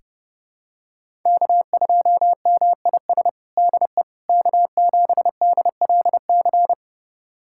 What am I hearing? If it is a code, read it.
Morse code, heard as a keyed tone, about 30 wpm, 700 Hz. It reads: K2MIS DE K7DLC
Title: USA Callsign DE Callsign - 5 Characters - 30wpm